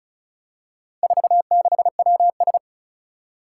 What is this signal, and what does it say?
Morse code, heard as a keyed tone, about 36 wpm, 700 Hz. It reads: 46WS
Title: Four Challenging Letter-Number Combinations 35wpm